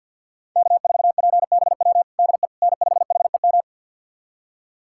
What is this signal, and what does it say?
Morse code, heard as a keyed tone, about 53 wpm, 700 Hz. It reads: K4PBW DE N5HEM
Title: USA Callsign DE Callsign - 5 Characters - 50wpm